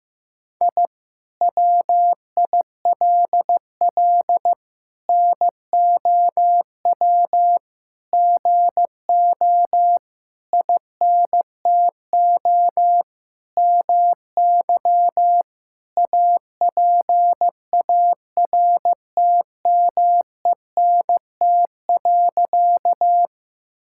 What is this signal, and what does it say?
Morse code, heard as a keyed tone, about 16 wpm, 700 Hz. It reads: I WILL NOW GO INTO MY APARTMENT.
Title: Sentences from Top 1000 Words - Review Part 6 - 15wpm